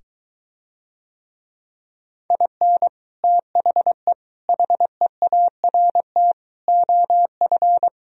Morse code, heard as a keyed tone, about 23 wpm, 700 Hz. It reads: IN THE HEART OF